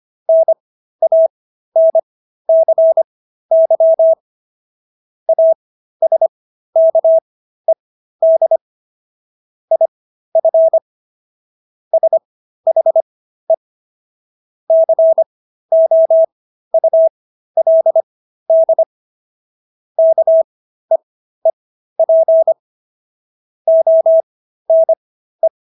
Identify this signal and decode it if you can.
Morse code, heard as a keyed tone, about 25 wpm, spaced as for 13 wpm, 650 Hz. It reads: NANCY ASKED IF SHE COULD KEEP ONE